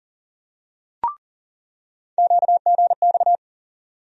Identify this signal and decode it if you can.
Morse code, heard as a keyed tone, about 40 wpm, 700 Hz. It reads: QGX
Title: USA Call Sign Postfixes - 3 Letters 40wpm